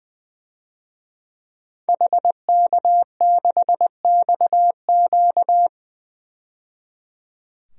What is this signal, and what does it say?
Morse code, heard as a keyed tone, about 21 wpm, 700 Hz. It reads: HK6XQ
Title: DXCC-100 Call Signs - 5 Characters - 20wpm